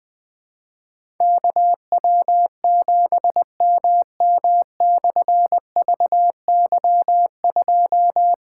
Morse code, heard as a keyed tone, about 20 wpm, 700 Hz. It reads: KW7MM/VY2
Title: Difficult Call Signs 20wpm